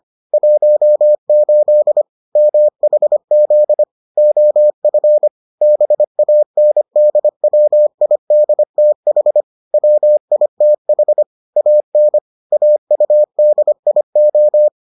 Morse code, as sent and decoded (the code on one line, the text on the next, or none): .---- ---.. / -- .... --.. / --- ..-. / -... .- -. -.. .-- .. -.. - .... / .-- .. - .... / .- -. / .- ..- -.. .. ---
18 MHZ OF BANDWIDTH WITH AN AUDIO